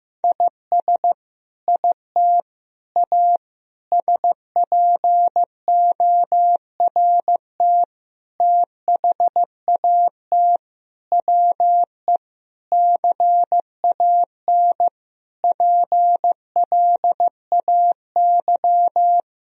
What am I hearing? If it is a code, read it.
Morse code, heard as a keyed tone, about 15 wpm, 700 Hz. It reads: IS IT A SPORT THAT WE CAN PLAY
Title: Sentences from Top 1000 Words - Review Part 4 - 15wpm